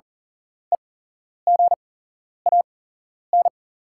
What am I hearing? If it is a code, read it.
Morse code, heard as a keyed tone, about 40 wpm, spaced as for 11 wpm, 700 Hz. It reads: EGAN
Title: 8x Character Spacing - Top 300 Words - 40wpm